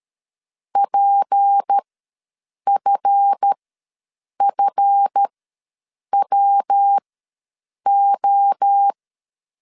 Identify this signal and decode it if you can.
Morse code, heard as a keyed tone, about 13 wpm, spaced as for 7 wpm, 800 Hz. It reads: PFFWO
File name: LEZIONI / 7 GF9Z / 7 GF ws 12wpm 3140